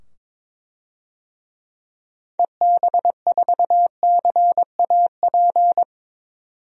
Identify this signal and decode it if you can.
Morse code, heard as a keyed tone, about 21 wpm, 700 Hz. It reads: EB4CAP
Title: DXCC-100 Call Signs - 6 Characters - 22wpm